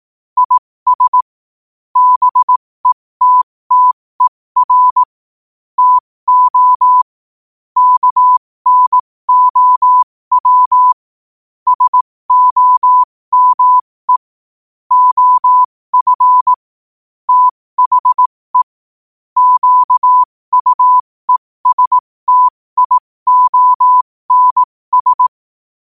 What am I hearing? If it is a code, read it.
Morse code, heard as a keyed tone, about 18 wpm, spaced as for 15 wpm, 1000 Hz. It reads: IS BETTER TO KNOW SOME OF THE QUESTIONS
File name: qotdfiles / 15wpm